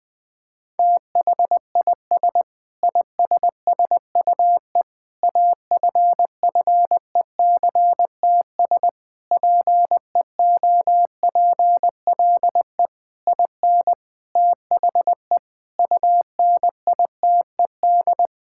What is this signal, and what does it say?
Morse code, heard as a keyed tone, about 20 wpm, 700 Hz. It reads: THIS ISSUE AFFECTS PEOPLE IN THE UNITED
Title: Sentences with 8-Word Phrases - 20wpm